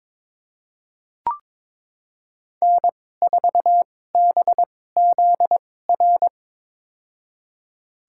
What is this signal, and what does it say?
Morse code, heard as a keyed tone, about 22 wpm, spaced as for 16 wpm, 700 Hz. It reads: N4BZR
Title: USA Call Signs - 2x Character Spacing - 22wpm